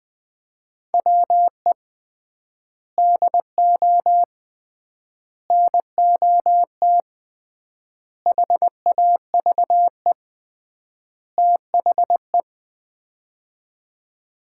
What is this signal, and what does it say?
Morse code, heard as a keyed tone, about 21 wpm, 700 Hz. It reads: WE DO NOT HAVE THE
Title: 3x Word-Spacing - Sets of 5 Words - 20wpm